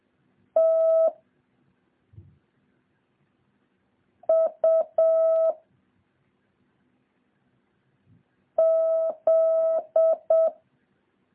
Morse code, heard as a keyed tone, about 7 wpm, 650 Hz. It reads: T U Z
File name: jh8cbh / 2023cw